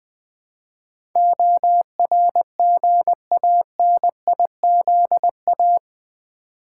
Morse code, heard as a keyed tone, about 20 wpm, 700 Hz. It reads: ORGANIZA
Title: Build A Word - 4 Letters At A Time - 12-Letter Words - 20wpm